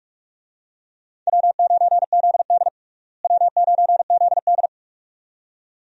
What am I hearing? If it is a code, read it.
Morse code, heard as a keyed tone, about 46 wpm, 700 Hz. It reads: W9ZD W9ZD
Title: USA Call Signs - 4 Characters - Repeated 2x - 45wpm